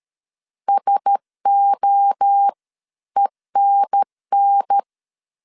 Morse code, heard as a keyed tone, about 13 wpm, spaced as for 11 wpm, 800 Hz. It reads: SOENN
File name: CW / DET / 1 6 1500C S B 12wpm 3431